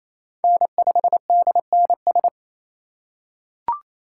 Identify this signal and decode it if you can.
Morse code, heard as a keyed tone, about 28 wpm, 700 Hz. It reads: N5DNS
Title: USA Call Signs - 5 Characters - 8x Character Spacing - 28wpm